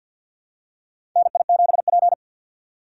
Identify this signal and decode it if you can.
Morse code, heard as a keyed tone, about 47 wpm, 700 Hz. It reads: NI7P
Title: USA Call Signs - 4 Characters - 6x Character Spacing - 50wpm